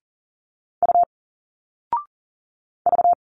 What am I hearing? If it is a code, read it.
Morse code, heard as a keyed tone, about 40 wpm, 700 Hz. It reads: U V
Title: Single Letters 40wpm